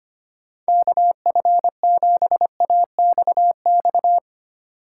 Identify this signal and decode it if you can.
Morse code, heard as a keyed tone, about 25 wpm, 700 Hz. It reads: KF7AXX
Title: USA Call Signs - 6 Characters - 4x Character Spacing - 25wpm